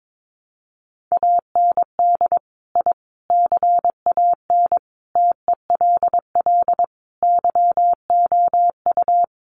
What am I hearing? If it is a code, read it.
Morse code, heard as a keyed tone, about 22 wpm, 700 Hz. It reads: AND I CAN TELL YOU
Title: Sets of 5 Words Encore2 22wpm